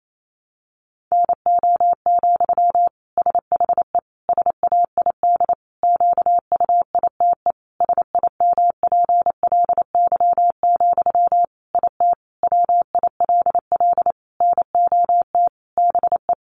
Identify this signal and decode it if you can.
Morse code, heard as a keyed tone, about 28 wpm, 700 Hz. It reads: NO, SHE SAID QUITE SIMPLY, IT WILL NOT BE
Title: Sentences from Top 800 Words 28wpm